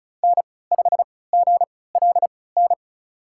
Morse code, heard as a keyed tone, about 34 wpm, spaced as for 20 wpm, 700 Hz. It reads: N5GLN
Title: USA Call Signs - 5 Characters - 3x Character Spacing - 35wpm